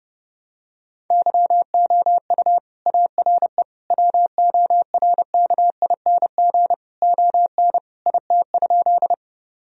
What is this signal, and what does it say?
Morse code, heard as a keyed tone, about 30 wpm, 700 Hz. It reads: YOU ARE WORKING ON IT?
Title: Sentences from Top 1000 Words - Review Part 4 - 30wpm